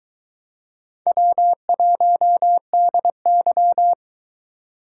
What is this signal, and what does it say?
Morse code, heard as a keyed tone, about 23 wpm, 700 Hz. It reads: W1DY